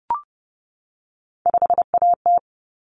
Morse code, heard as a keyed tone, about 30 wpm, 700 Hz. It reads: 5AT
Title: Common Cut Numbers - 3-Digit 30wpm